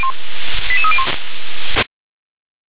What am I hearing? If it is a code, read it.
Five-tone selective call (ZVEI), heard as quick runs of five tones, tones of about 70 ms. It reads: 88301